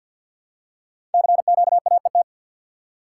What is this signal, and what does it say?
Morse code, heard as a keyed tone, about 48 wpm, 700 Hz. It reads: CQAET
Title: Code Groups - 5 Letter:Numbers - 50wpm